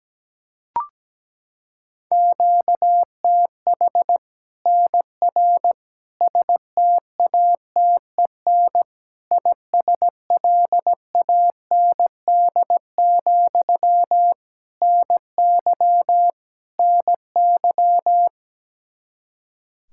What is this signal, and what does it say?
Morse code, heard as a keyed tone, about 17 wpm, 700 Hz. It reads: QTH NR STATEN ISLAND, NY NY
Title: Practice QSO - Exchange 1 - Beginning - 17wpm